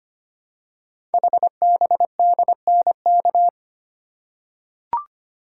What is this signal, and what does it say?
Morse code, heard as a keyed tone, about 25 wpm, 700 Hz. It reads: HBDNK
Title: Code Groups - 5 Letters - 25wpm